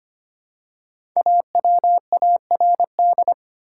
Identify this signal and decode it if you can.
Morse code, heard as a keyed tone, about 25 wpm, 700 Hz. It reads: AWARD
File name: 5 Letter Words - First 3 Letters Given - 25wpm